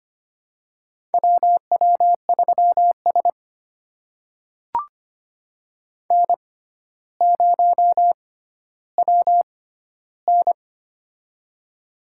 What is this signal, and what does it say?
Morse code, heard as a keyed tone, about 25 wpm, spaced as for 13 wpm, 700 Hz. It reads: WW3S N0WN
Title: USA Call Signs - 4 Characters - 6x Character Spacing - 25wpm